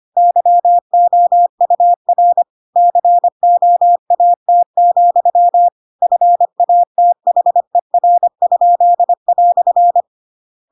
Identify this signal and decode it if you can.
Morse code, heard as a keyed tone, about 25 wpm, 700 Hz. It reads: YOUR COAT, FATHER?"